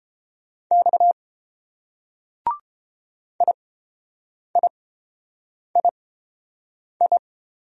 Morse code, heard as a keyed tone, about 27 wpm, spaced as for 7 wpm, 700 Hz. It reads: X IIII
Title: Speed-Racing Single Letters 22wpm